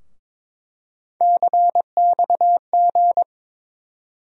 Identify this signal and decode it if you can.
Morse code, heard as a keyed tone, about 22 wpm, 700 Hz. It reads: CXG
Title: USA Call Sign Postfixes - 3 Letters 22wpm